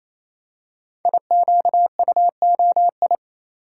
Morse code, heard as a keyed tone, about 27 wpm, 690 Hz. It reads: IQUOI
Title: Code Groups - 5 Letters - 28wpm